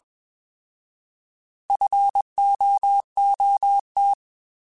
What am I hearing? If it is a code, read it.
Morse code, heard as a keyed tone, about 21 wpm, 800 Hz. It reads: FOOT
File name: Alphabetical / Alphabetical mp3s / Fs @20